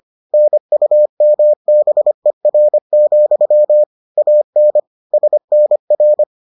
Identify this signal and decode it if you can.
Morse code, heard as a keyed tone, about 25 wpm, 600 Hz. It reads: NUMBER, AN SNR